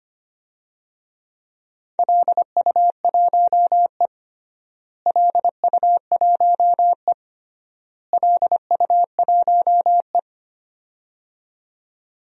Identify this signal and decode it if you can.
Morse code, heard as a keyed tone, about 25 wpm, 700 Hz. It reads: LU1E LU1E LU1E
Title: DXCC-100 Call Signs - 4 Characters - Repeated 3x - 25wpm